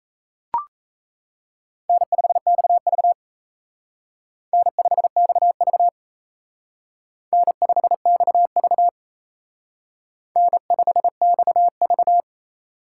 Morse code, heard as a keyed tone, about 34 wpm, 700 Hz. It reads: N5XV N5XV N5XV N5XV
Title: Speed Racing - USA Call Signs 28wpm